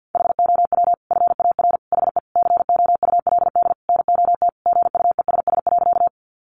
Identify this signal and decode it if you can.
Morse code, heard as a keyed tone, about 50 wpm, 700 Hz. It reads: HOW FAR HE COULD NOT GUESS.
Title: Sentences from Top 1500 Words 50wpm